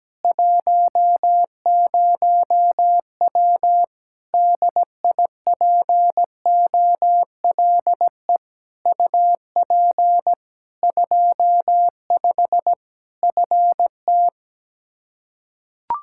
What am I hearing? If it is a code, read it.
Morse code, heard as a keyed tone, about 17 wpm, 700 Hz. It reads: 10W DIPOLE UP 25 FT
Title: K1USN SST QSO Practice 17wpm